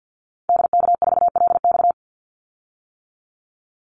Morse code, heard as a keyed tone, about 50 wpm, 700 Hz. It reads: DK4LX
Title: Call Signs - 50wpm